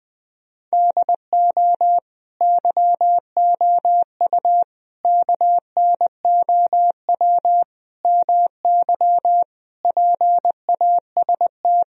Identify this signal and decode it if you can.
Morse code, heard as a keyed tone, about 20 wpm, 700 Hz. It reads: DO YOU KNOW MY PAST
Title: Sentences from Top 1000 Words - Review Part 3 - 20wpm